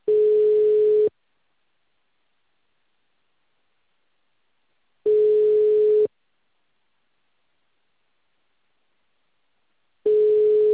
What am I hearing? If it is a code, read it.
Ringback tone (Europe, 425 Hz).